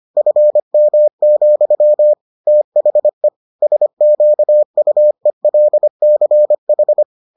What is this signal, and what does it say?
Morse code, heard as a keyed tone, about 25 wpm, 600 Hz. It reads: FM, THE SQUELCH